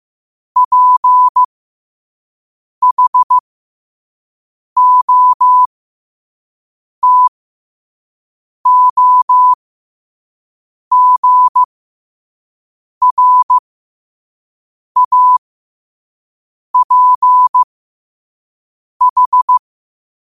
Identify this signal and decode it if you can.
Morse code, heard as a keyed tone, about 15 wpm, spaced as for 5 wpm, 1000 Hz. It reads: PHOTOGRAPH
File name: 5wpm